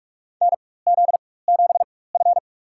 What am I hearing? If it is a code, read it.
Morse code, heard as a keyed tone, about 46 wpm, spaced as for 21 wpm, 700 Hz. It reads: NZ7F
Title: USA Call Signs - 4 Characters - 4x Character Spacing - 45wpm